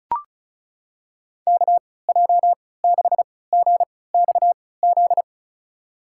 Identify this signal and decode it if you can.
Morse code, heard as a keyed tone, about 34 wpm, spaced as for 20 wpm, 700 Hz. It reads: KJ6GXZ
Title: USA Call Signs - 3x Character Spacing - 35wpm